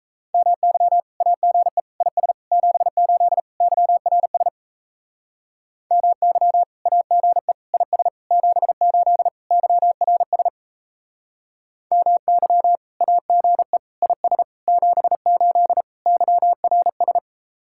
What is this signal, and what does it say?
Morse code, heard as a keyed tone, about 37 wpm, 700 Hz. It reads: MY AGE IS 78 YRS MY AGE IS 78 YRS MY AGE IS 78 YRS